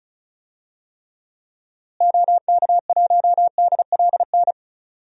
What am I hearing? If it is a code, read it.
Morse code, heard as a keyed tone, about 34 wpm, 700 Hz. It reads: OK1DLN
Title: DXCC-100 Call Signs - 6 Characters - Repeated 3x - 35wpm